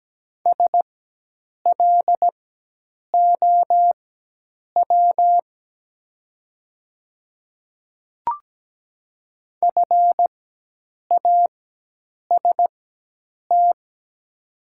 Morse code, heard as a keyed tone, about 17 wpm, spaced as for 8 wpm, 700 Hz. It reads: SLOW FAST
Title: 4x Character Spacing - Top 500 Words - 17wpm